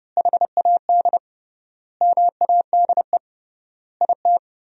Morse code, heard as a keyed tone, about 30 wpm, 700 Hz. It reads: HAD MADE IT
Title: 3x Word-Spacing - Sentences from Top 500 Words - 30wpm